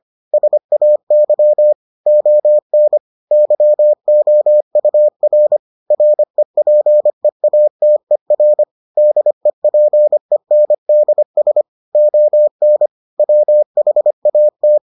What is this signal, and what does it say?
Morse code, heard as a keyed tone, about 25 wpm, 600 Hz. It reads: SAY ON YOUR REPEATER DEPENDS ON WHAT